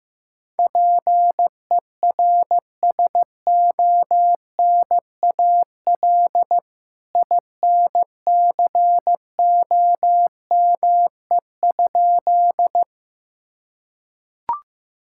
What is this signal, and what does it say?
Morse code, heard as a keyed tone, about 15 wpm, 700 Hz. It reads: PERSONAL INCOME?
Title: Sentences from Top 1000 Words - Review Part 5 - 15wpm